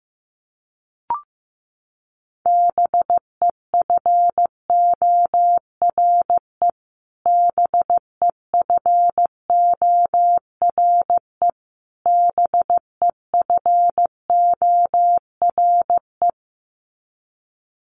Morse code, heard as a keyed tone, about 15 wpm, 700 Hz. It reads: BEFORE BEFORE BEFORE
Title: Top 300 Words Repeated 15wpm